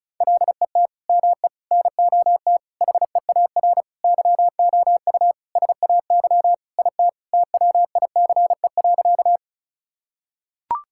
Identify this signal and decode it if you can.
Morse code, heard as a keyed tone, about 36 wpm, 700 Hz. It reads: LET ME NOT HEAR YOU SAY IT TWICE.
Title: Sentences from Top 1600 Words 35wpm